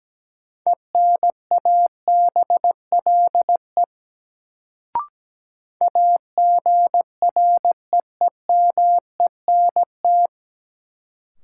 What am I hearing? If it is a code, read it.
Morse code, heard as a keyed tone, about 17 wpm, 700 Hz. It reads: ENABLE AGREEMENT
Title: Sentences from Top 1500 Words - New Word Drill - 17wpm